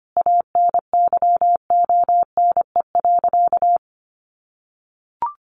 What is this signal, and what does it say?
Morse code, heard as a keyed tone, about 25 wpm, 700 Hz. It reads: ANYONE.
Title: Sentences from Top 900 Words 25wpm